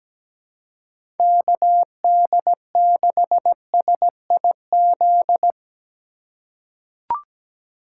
Morse code, heard as a keyed tone, about 17 wpm, 700 Hz. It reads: KD6SIZ
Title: USA Call Signs - 6 Characters - 2x Character Spacing - 17wpm